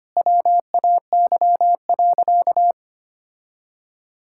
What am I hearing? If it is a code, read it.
Morse code, heard as a keyed tone, about 25 wpm, 700 Hz. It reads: WAY.
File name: practice / 25wpm / Sentences with 9-Word Phrases - 25wpm